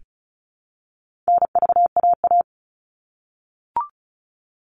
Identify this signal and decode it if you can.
Morse code, heard as a keyed tone, about 34 wpm, 700 Hz. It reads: NVAA
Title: All Cut Numbers - 4-Digit 35wpm